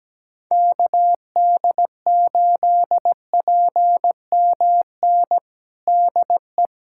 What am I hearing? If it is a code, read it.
Morse code, heard as a keyed tone, about 17 wpm, 700 Hz. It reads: KD8PMN DE
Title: USA Callsign DE Callsign - 6 Characters - 17wpm